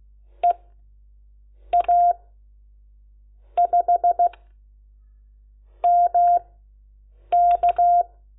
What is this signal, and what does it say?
Morse code, heard as a keyed tone, about 16 wpm, spaced as for 6 wpm, 700 Hz. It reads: EA5MK
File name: Lecciones